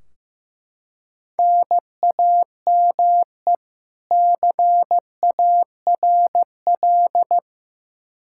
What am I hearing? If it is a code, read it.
Morse code, heard as a keyed tone, about 15 wpm, 700 Hz. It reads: NAME CARL
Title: Practice QSO - Exchange 1 - Beginning - 15wpm